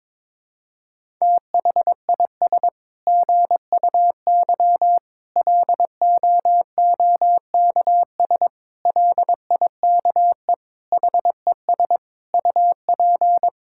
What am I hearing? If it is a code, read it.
Morse code, heard as a keyed tone, about 22 wpm, 700 Hz. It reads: THIS GUY LOOKS LIKE HES UP
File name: Sets of 9 Words - 22wpm